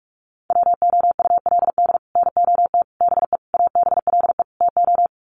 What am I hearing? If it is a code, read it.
Morse code, heard as a keyed tone, about 44 wpm, 700 Hz. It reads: WOULD NOT BE ABLE TO
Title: Sets of 5 Words 45wpm